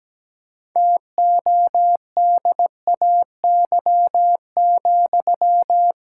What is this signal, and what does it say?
Morse code, heard as a keyed tone, about 17 wpm, 700 Hz. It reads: TODAY,
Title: Sentences from Top 1600 Words 17wpm